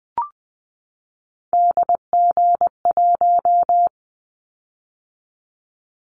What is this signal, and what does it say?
Morse code, heard as a keyed tone, about 20 wpm, 700 Hz. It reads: DG1
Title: Callsign Prefixes for Top 25 DXCC - 20wpm